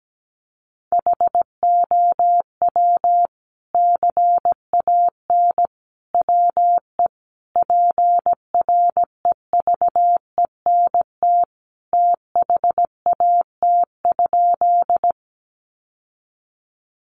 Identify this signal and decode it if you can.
Morse code, heard as a keyed tone, about 17 wpm, 700 Hz. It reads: HOW CAN WE PREVENT THAT?
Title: Sentences from Top 1100 Words 17wpm